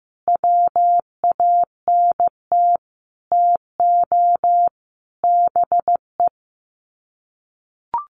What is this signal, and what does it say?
Morse code, heard as a keyed tone, about 15 wpm, 700 Hz. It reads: WANT TO BE
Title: Sets of 5 Words Encore 15wpm